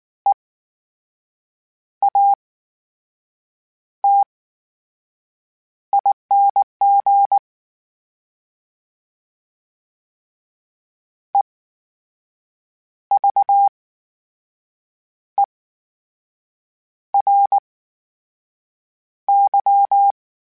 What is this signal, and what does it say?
Morse code, heard as a keyed tone, about 19 wpm, spaced as for 6 wpm, 800 Hz. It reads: EATING EVERY
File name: Special INGs / Special INGs-mp3s / VerySlowToFaster Ing @18